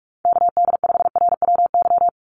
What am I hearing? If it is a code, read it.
Morse code, heard as a keyed tone, about 44 wpm, 700 Hz. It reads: KD5RWY
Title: USA Call Signs 45wpm